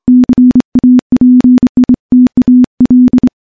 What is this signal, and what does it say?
Morse code, heard as a keyed tone, about 26 wpm, 250 Hz. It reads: CAPITAL
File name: morse-training-25WPM